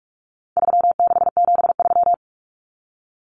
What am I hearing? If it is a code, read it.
Morse code, heard as a keyed tone, about 44 wpm, 700 Hz. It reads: 3673
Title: Common Cut Numbers - 4-Digit 45wpm